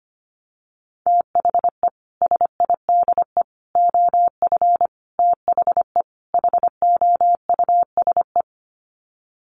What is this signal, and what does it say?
Morse code, heard as a keyed tone, about 25 wpm, 700 Hz. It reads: THE SIDE OF THE HOUSE
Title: Sets of 5 Words Encore2 25wpm